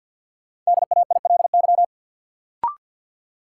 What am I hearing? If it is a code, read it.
Morse code, heard as a keyed tone, about 52 wpm, 700 Hz. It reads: DAILY
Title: Sentences from Top 1100 Words - New Word Drill Alt - 50wpm